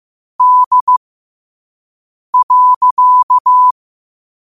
Morse code, heard as a keyed tone, about 15 wpm, spaced as for 5 wpm, 1000 Hz. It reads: D.
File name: qotdfiles / 5wpm